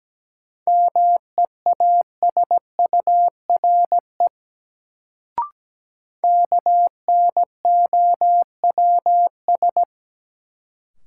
Morse code, heard as a keyed tone, about 17 wpm, 700 Hz. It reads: MEASURE KNOWS
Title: Sentences from Top 1000 Words - Review Part 1 - New Word Drill - 17wpm